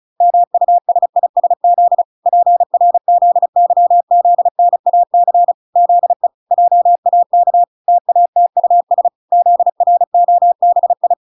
Morse code, heard as a keyed tone, about 34 wpm, 700 Hz. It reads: MUSISZ PRZYZNAC ZE JAK TATUS ZROBI